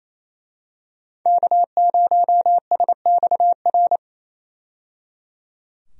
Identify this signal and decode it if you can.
Morse code, heard as a keyed tone, about 27 wpm, 700 Hz. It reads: K0SXR
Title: USA Call Signs - 5 Characters - 28wpm